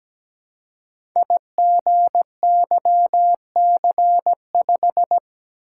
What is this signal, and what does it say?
Morse code, heard as a keyed tone, about 17 wpm, 700 Hz. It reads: IGYC5
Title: Code Groups - 5 Letter:Numbers - 17wpm